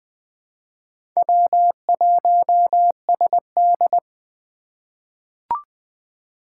Morse code, heard as a keyed tone, about 20 wpm, 700 Hz. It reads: W1SD
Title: USA Call Signs - 4 Characters - 6x Character Spacing - 20wpm